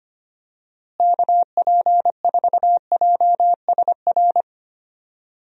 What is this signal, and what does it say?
Morse code, heard as a keyed tone, about 25 wpm, 700 Hz. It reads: KP4JSR
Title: USA Call Signs - 6 Characters - Repeated 2x - 25wpm